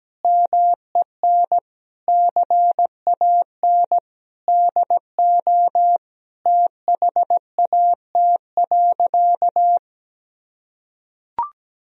Morse code, heard as a keyed tone, about 17 wpm, 700 Hz. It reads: MEN CAN DO THAT.
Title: Sentences from Top 1000 Words - Review Part 1 - 17wpm